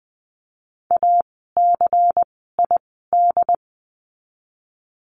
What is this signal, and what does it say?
Morse code, heard as a keyed tone, about 20 wpm, spaced as for 14 wpm, 700 Hz. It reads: ACID
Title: 094 Lesson 16f - Letter c - Words 20wpm